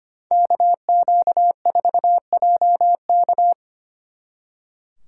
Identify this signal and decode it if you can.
Morse code, heard as a keyed tone, about 32 wpm, 700 Hz. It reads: KQ4JK